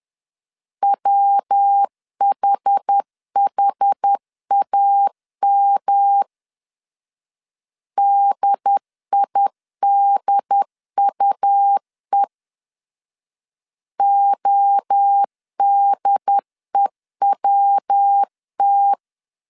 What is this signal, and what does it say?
Morse code, heard as a keyed tone, about 10 wpm, 800 Hz. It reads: WHHAM DIDUE ODEWT
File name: CW / DET / 1 3 1500C S C 10wpm 3335